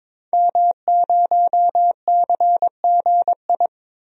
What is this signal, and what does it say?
Morse code, heard as a keyed tone, about 22 wpm, 700 Hz. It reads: M0CGI
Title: DXCC-100 Call Signs - 5 Characters - 22wpm